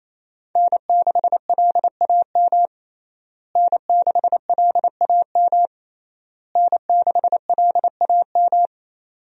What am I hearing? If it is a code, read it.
Morse code, heard as a keyed tone, about 28 wpm, 700 Hz. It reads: N6LAM N6LAM N6LAM
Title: USA Call Signs - 5 Characters - Repeated 3x - 28wpm